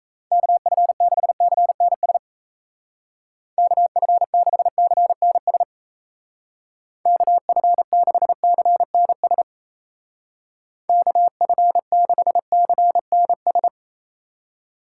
Morse code, heard as a keyed tone, about 35 wpm, 700 Hz. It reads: KF6CNS KF6CNS KF6CNS KF6CNS
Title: Speed Racing - USA 2x3 Call Signs 28wpm